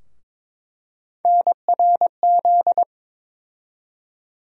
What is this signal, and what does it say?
Morse code, heard as a keyed tone, about 22 wpm, 700 Hz. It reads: NRZ
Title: USA Call Sign Postfixes - 3 Letters 22wpm